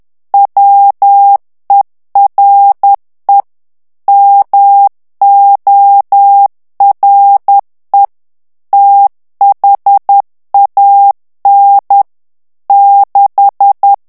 Morse code, heard as a keyed tone, about 10 wpm, 800 Hz. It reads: WERE MORE THAN 6
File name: ANGLAIS texte clair 3 vitesse 10 mots minute